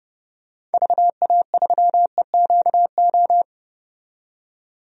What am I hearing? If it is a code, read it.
Morse code, heard as a keyed tone, about 30 wpm, 700 Hz. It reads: VA3EQO
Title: DXCC-100 Call Signs - 6 Characters - 30wpm